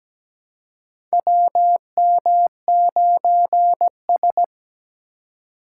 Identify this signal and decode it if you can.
Morse code, heard as a keyed tone, about 17 wpm, 700 Hz. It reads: WM9S
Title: USA Call Signs - 4 Characters - 3x Character Spacing - 17wpm